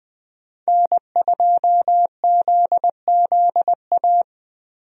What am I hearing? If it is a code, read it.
Morse code, heard as a keyed tone, about 20 wpm, 700 Hz. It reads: N2ZZA
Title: USA Call Signs - 5 Characters - Repeated 2x - 20wpm